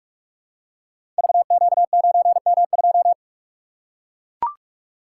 Morse code, heard as a keyed tone, about 44 wpm, 700 Hz. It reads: VQ9M2
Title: Code Groups - 5 Letter:Numbers - 45wpm